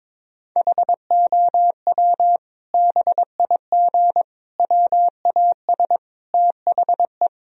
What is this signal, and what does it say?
Morse code, heard as a keyed tone, about 22 wpm, 700 Hz. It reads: HOW BIG WAS THE
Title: Sentences from Top 1000 Words - Review Part 4 - 22wpm